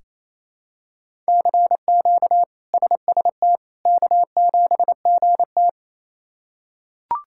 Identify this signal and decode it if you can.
Morse code, heard as a keyed tone, about 27 wpm, 700 Hz. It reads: CQ SST K7GT
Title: K1USN SST QSO Practice - 28wpm